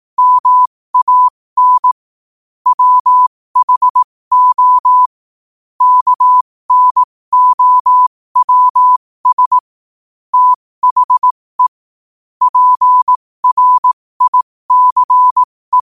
Morse code, heard as a keyed tone, about 18 wpm, spaced as for 15 wpm, 1000 Hz. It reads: MAN WHO KNOWS THE PRICE
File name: qotdfiles / 15wpm